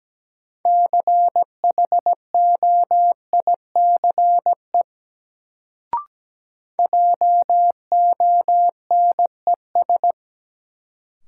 Morse code, heard as a keyed tone, about 17 wpm, 700 Hz. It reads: CHOICE JONES
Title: Sets of 5 Words - New Word Drill - 17wpm